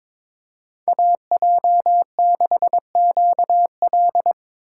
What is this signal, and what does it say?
Morse code, heard as a keyed tone, about 21 wpm, 700 Hz. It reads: AJ6QL
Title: K1USN SST QSO Practice - 22wpm